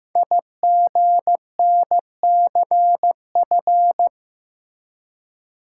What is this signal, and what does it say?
Morse code, heard as a keyed tone, about 15 wpm, 700 Hz. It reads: IGNCF
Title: Code Groups - 5 Letters - 15wpm